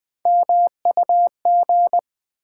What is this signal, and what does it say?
Morse code, heard as a keyed tone, about 20 wpm, 690 Hz. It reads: MUG
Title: Difficult Words with Hints - 20wpm